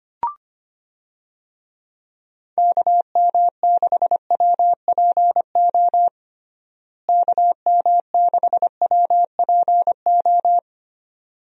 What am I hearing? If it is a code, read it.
Morse code, heard as a keyed tone, about 25 wpm, 700 Hz. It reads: KM6WPO KM6WPO
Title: USA Call Signs - 6 Characters - Repeated 2x - 25wpm